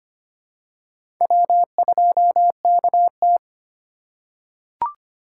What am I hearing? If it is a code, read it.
Morse code, heard as a keyed tone, about 25 wpm, 700 Hz. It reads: W2KT
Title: USA Call Signs - 4 Characters - 3x Character Spacing - 25wpm